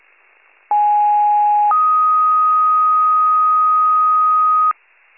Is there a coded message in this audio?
Yes: two-tone page from 0.7 to 4.7 s, heard as a one-second tone, then a three-second tone, about 830 then 1250 Hz.